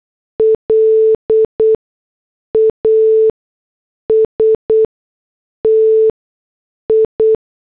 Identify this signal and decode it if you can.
Morse code, heard as a keyed tone, about 8 wpm, spaced as for 6 wpm, 440 Hz. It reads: LASTI